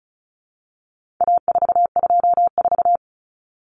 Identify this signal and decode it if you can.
Morse code, heard as a keyed tone, about 34 wpm, 700 Hz. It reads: A424
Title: Common Cut Numbers - 4-Digit 35wpm